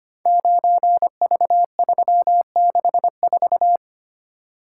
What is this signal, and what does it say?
Morse code, heard as a keyed tone, about 25 wpm, 700 Hz. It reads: 9V364
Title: Code Groups - 5 Letter:Numbers - 25wpm